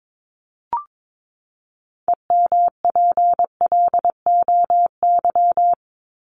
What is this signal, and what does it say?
Morse code, heard as a keyed tone, about 22 wpm, 700 Hz. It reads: EMPLOY
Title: ED Suffix - Top 100 Words with Base Words 22wpm